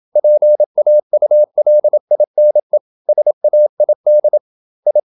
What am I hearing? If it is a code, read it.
Morse code, heard as a keyed tone, about 27 wpm, 600 Hz. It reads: PAULINE SAID I
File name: Never 0027wpm